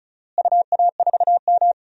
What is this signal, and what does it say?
Morse code, heard as a keyed tone, about 36 wpm, 700 Hz. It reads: UA4M
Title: DXCC-100 Call Signs - 4 Characters - Repeated 2x - 35wpm